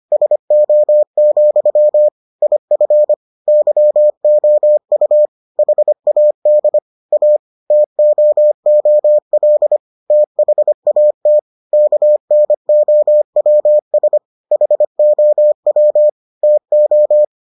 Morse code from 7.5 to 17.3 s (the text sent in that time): TOOL THAT KNOWS HOW TO